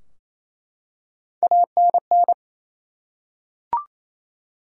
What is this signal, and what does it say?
Morse code, heard as a keyed tone, about 28 wpm, 700 Hz. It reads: ANN
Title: USA Common Names - Max 3 Letters - 28wpm